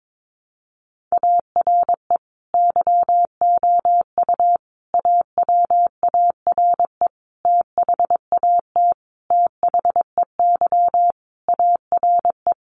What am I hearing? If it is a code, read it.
Morse code, heard as a keyed tone, about 22 wpm, 700 Hz. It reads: ARE YOU AWARE THAT THEY ARE
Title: Sentences from Top 1200 Words 22wpm